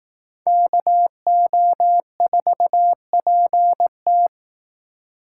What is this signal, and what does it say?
Morse code, heard as a keyed tone, about 18 wpm, 700 Hz. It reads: KO4PT
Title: Speed Racing - USA Call Signs 15wpm